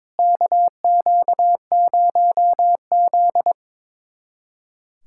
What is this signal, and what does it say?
Morse code, heard as a keyed tone, about 22 wpm, 700 Hz. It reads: KQ0Z
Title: USA Callsign DE Callsign - 4 Characters - 22wpm